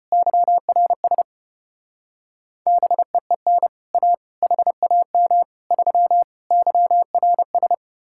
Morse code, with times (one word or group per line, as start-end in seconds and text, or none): 0.1-1.2 YRS
2.7-3.7 BEEN
3.9-4.1 A
4.4-5.4 HAM
5.7-6.2 3
6.5-7.7 YRS